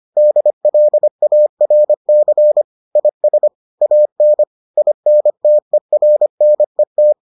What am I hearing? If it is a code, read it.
Morse code, heard as a keyed tone, about 25 wpm, 600 Hz. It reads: DLARC IS AN INTERNET